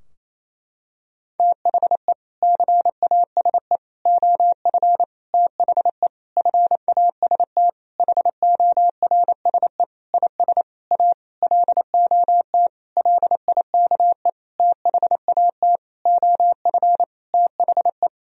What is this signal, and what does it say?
Morse code, heard as a keyed tone, about 28 wpm, 700 Hz. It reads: THE CASE OF THE FAST HORSE IS A LOT LIKE THAT OF THE
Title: Sentences from Top 1000 Words - Review Part 2 - 28wpm